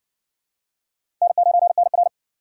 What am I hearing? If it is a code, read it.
Morse code, heard as a keyed tone, about 60 wpm, 700 Hz. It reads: N9NR